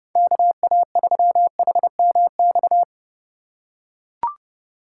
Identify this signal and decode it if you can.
Morse code, heard as a keyed tone, about 32 wpm, 700 Hz. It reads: KA3HMX
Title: USA Call Signs - 6 Characters - 2x Character Spacing - 30wpm